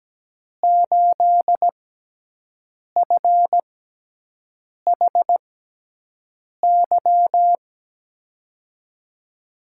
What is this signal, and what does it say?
Morse code, heard as a keyed tone, about 17 wpm, spaced as for 6 wpm, 700 Hz. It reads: 8FHY
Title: USA Call Signs - 5 Characters - 6x Character Spacing - 17wpm